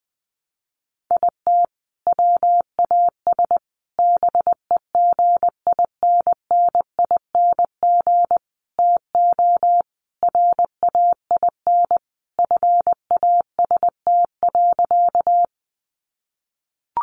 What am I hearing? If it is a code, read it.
Morse code, heard as a keyed tone, about 20 wpm, 700 Hz. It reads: IT WAS BEGINNING TO RAIN FAST.
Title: Sentences from Top 800 Words 20wpm